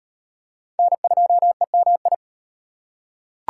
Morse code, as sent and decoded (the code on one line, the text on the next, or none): -. ..--- . -- ..
N2EMI